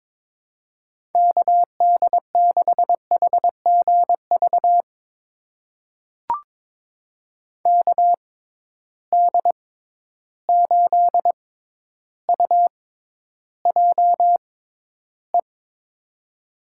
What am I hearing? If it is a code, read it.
Morse code, heard as a keyed tone, about 22 wpm, spaced as for 11 wpm, 700 Hz. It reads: KD6HGV KD8UJE
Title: USA Call Signs - 6 Characters - 6x Character Spacing - 22wpm